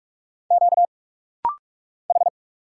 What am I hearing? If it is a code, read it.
Morse code, heard as a keyed tone, about 44 wpm, 700 Hz. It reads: Q H
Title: Single Letter-Number Alt - Rapid-Fire 45wpm